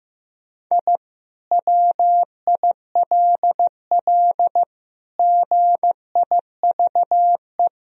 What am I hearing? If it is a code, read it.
Morse code, heard as a keyed tone, about 16 wpm, 700 Hz. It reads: I WILL GIVE
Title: Sentences from Top 1000 Words - Review Part 2 - 15wpm